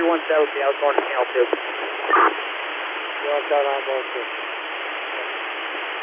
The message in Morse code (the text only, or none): none